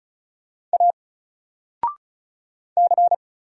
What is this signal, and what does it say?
Morse code, heard as a keyed tone, about 34 wpm, 700 Hz. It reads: A C
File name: Single-Letter-Number-35wpm